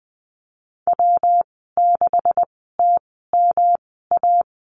Morse code, heard as a keyed tone, about 20 wpm, spaced as for 14 wpm, 700 Hz. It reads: W6TMA
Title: 128 Lesson 21g - Number 6 - Call Signs 20wpm